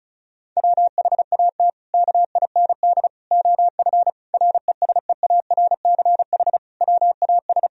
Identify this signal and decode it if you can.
Morse code, heard as a keyed tone, about 34 wpm, 700 Hz. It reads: WHAT KIND OF RESEARCH WAS
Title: Sentences from Top 1000 Words - Review Part 4 - 35wpm